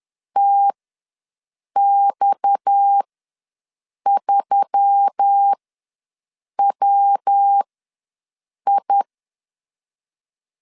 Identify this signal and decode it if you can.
Morse code, heard as a keyed tone, about 10 wpm, spaced as for 6 wpm, 800 Hz. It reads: TX3WI
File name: LEZIONI / 6 PKX0 / 6 PKX ws 10wpm 3706